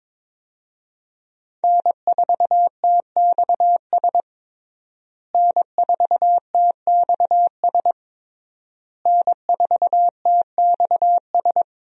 Morse code, heard as a keyed tone, about 22 wpm, 700 Hz. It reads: N4TXS N4TXS N4TXS
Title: USA Call Signs - 5 Characters - Repeated 3x - 22wpm